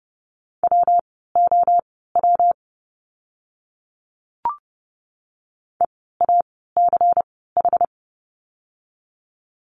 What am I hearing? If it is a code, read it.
Morse code, heard as a keyed tone, about 30 wpm, spaced as for 17 wpm, 700 Hz. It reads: WOW EACH
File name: course / 30:10wpm / 115 Lesson 19f - Letter w - Words 30wpm